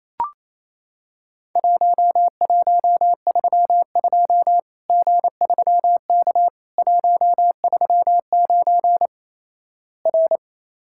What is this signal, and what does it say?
Morse code, heard as a keyed tone, about 28 wpm, 700 Hz. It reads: 1132 G3K 139 R
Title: Worked All Europe DX Contest - QTC Exchange - 28wpm